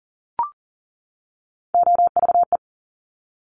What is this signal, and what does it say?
Morse code, heard as a keyed tone, about 39 wpm, 700 Hz. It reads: OVE
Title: Top Three Letter Combinations in Words 40wpm